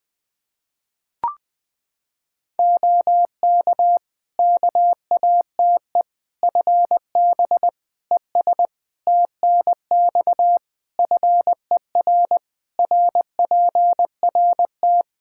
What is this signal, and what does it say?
Morse code, heard as a keyed tone, about 20 wpm, 700 Hz. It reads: OK KATE FB ES TNX FER RPRT
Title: Practice QSO - Exchange 2 - Info - 20wpm